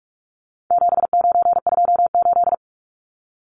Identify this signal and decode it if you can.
Morse code, heard as a keyed tone, about 44 wpm, 700 Hz. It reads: 7928
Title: Numbers - 4-Digit 45wpm